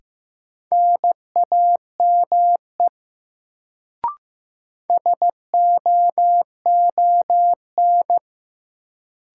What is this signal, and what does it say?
Morse code, heard as a keyed tone, about 15 wpm, 700 Hz. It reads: NAME SOON
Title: Sets of 3 Words - New Word Drill - 15wpm